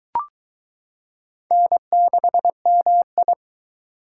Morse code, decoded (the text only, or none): N6MI